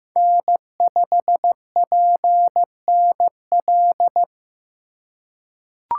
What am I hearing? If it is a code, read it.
Morse code, heard as a keyed tone, about 15 wpm, 700 Hz. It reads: N5PNL
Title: USA Call Signs - 4x Character Spacing - 15wpm